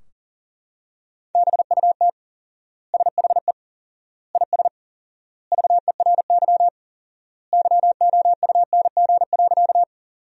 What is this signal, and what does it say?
Morse code, heard as a keyed tone, about 40 wpm, 700 Hz. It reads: BUT SHE IS VERY YOUNG.
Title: 4x Word-Spacing - Sentences from Top 400 Words - 40wpm